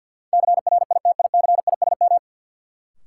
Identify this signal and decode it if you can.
Morse code, heard as a keyed tone, about 53 wpm, 700 Hz. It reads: CRITICISM
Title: Sentences from Top 1300 Words - New Word Drill Alt - 50wpm